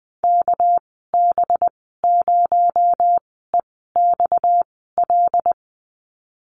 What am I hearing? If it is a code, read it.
Morse code, heard as a keyed tone, about 20 wpm, spaced as for 15 wpm, 700 Hz. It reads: KB0EXL
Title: 238 Lesson 38g - Letter q - Call Signs 20wpm